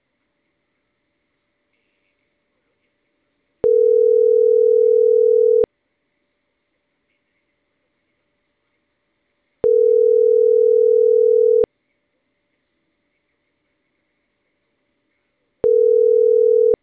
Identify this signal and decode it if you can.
Ringback tone (North America), 440 + 480 Hz.